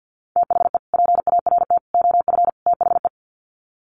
Morse code, heard as a keyed tone, about 50 wpm, 700 Hz. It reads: THE PART OF THE
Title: Sets of 4 Words 50wpm